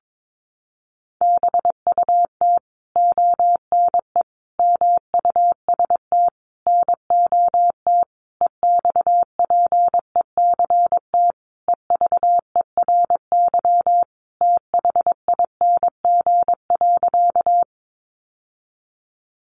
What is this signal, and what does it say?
Morse code, heard as a keyed tone, about 22 wpm, 700 Hz. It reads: BUT ONE MUST NOT EXPECT EVERYTHING.
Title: Sentences from Top 1100 Words 22wpm